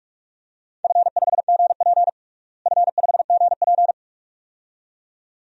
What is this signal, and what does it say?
Morse code, heard as a keyed tone, about 42 wpm, 700 Hz. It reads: F5GP F5GP
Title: DXCC-100 Call Signs - 4 Characters - Repeated 2x - 45wpm